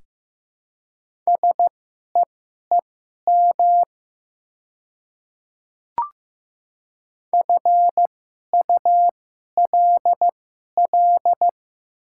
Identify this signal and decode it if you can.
Morse code, heard as a keyed tone, about 15 wpm, spaced as for 11 wpm, 700 Hz. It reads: SEEM FULL
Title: 2x Character Spacing - Top 500 Words - 15wpm